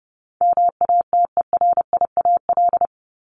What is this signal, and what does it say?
Morse code, heard as a keyed tone, about 30 wpm, 700 Hz. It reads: MATERIAL
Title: ES:S Suffix - Top 100 Words with Base Words 30wpm